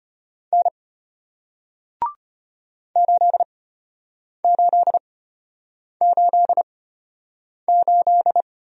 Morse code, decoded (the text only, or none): N 8888